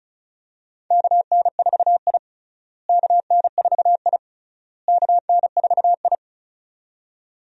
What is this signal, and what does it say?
Morse code, heard as a keyed tone, about 36 wpm, 700 Hz. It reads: KN4I KN4I KN4I
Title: USA Call Signs - 4 Characters - Repeated 3x - 35wpm